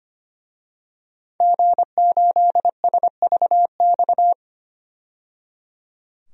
Morse code, heard as a keyed tone, about 25 wpm, 700 Hz. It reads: G8SVX
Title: DXCC-100 Call Signs - 5 Characters - 25wpm